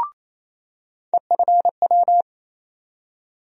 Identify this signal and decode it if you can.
Morse code, heard as a keyed tone, about 28 wpm, 700 Hz. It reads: EFW
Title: USA Call Sign Postfixes - 3 Letters 28wpm